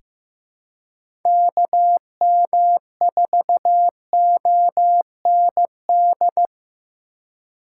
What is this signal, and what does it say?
Morse code, heard as a keyed tone, about 15 wpm, 700 Hz. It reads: KM4OND
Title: USA Call Signs - 4x Character Spacing - 15wpm